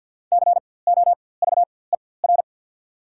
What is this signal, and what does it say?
Morse code, heard as a keyed tone, about 48 wpm, spaced as for 23 wpm, 700 Hz. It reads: COVER
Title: 4x Character Spacing - Top 300 Words - 50wpm